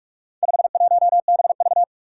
Morse code, heard as a keyed tone, about 44 wpm, 700 Hz. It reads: 51BV
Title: Four Challenging Letter-Number Combinations 45wpm